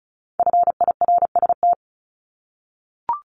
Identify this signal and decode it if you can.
Morse code, heard as a keyed tone, about 36 wpm, 700 Hz. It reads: FIRST
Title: Multi-letter Phonemes 35wpm